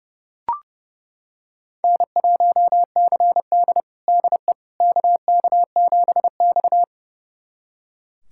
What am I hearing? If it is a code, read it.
Morse code, heard as a keyed tone, about 30 wpm, 700 Hz. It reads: N1CD DE KK7X
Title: USA Callsign DE Callsign - 4 Characters - 30wpm